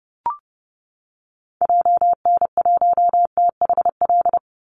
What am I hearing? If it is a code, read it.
Morse code, heard as a keyed tone, about 30 wpm, 700 Hz. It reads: JN1THL
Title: Call Signs - 30wpm